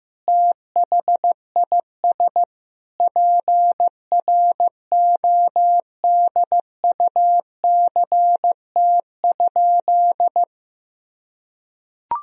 Morse code, heard as a keyed tone, about 15 wpm, 700 Hz. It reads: THIS PRODUCT?
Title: Sentences from Top 1000 Words - Review Part 3 - 15wpm